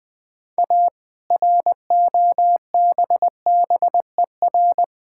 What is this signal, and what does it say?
Morse code, heard as a keyed tone, about 20 wpm, 700 Hz. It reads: A ROBBER
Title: All-Age Appropriate Jokes - 20wpm